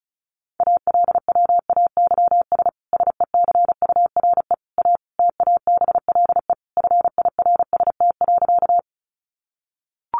Morse code, heard as a keyed tone, about 34 wpm, 700 Hz. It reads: ALWAYS SECURE A TABLE FIRST.
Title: Sentences from Top 1200 Words 35wpm